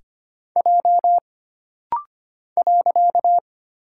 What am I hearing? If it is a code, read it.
Morse code, heard as a keyed tone, about 25 wpm, 700 Hz. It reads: J .
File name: Single Character (Letter:Number:Punctuation:Prosign) - Rapid-Fire - 25wpm